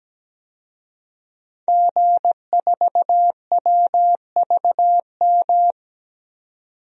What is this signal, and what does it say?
Morse code, heard as a keyed tone, about 17 wpm, 700 Hz. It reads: G4WVM
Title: DXCC-100 Call Signs - 5 Characters - 17wpm